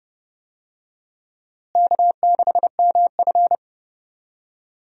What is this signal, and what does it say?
Morse code, heard as a keyed tone, about 30 wpm, 700 Hz. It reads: K6MF
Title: USA Call Signs - 4 Characters - Repeated 3x - 30wpm